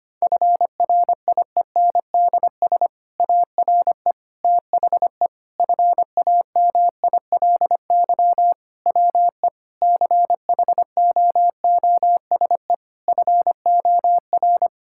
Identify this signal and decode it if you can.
Morse code, heard as a keyed tone, about 25 wpm, 700 Hz. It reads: FRIENDS ARE THE FAMILY WE CHOOSE FOR
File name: practice / 25wpm / Difficult Sentences with Hints - 25wpm